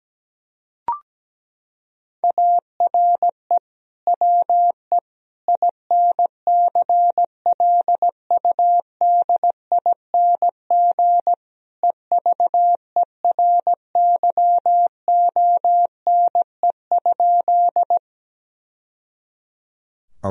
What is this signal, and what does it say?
Morse code, heard as a keyed tone, about 17 wpm, 700 Hz. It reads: ARE WE INCLUDING EVERYONE?
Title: Sentences from Top 1000 Words - Review Part 4 - 17wpm